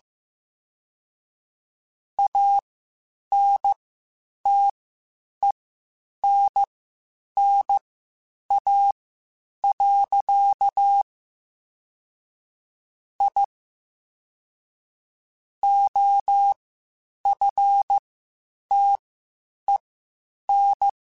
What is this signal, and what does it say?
Morse code, heard as a keyed tone, about 15 wpm, spaced as for 8 wpm, 800 Hz. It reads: ANTENNA. I OFTEN
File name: (spaced) / QSOs Spaced-mp3s / WA7QOX 2 @13 sp